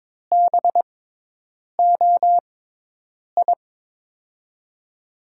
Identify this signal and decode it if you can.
Morse code, heard as a keyed tone, about 22 wpm, spaced as for 8 wpm, 700 Hz. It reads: BOI
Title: USA Call Signs - 6x Character Spacing - 22wpm